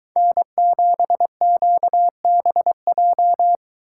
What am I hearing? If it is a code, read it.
Morse code, heard as a keyed tone, about 22 wpm, 700 Hz. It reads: N7QBJ